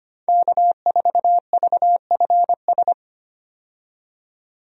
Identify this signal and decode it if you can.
Morse code, heard as a keyed tone, about 25 wpm, 700 Hz. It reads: K4VFS
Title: USA Call Signs - 5 Characters - 25wpm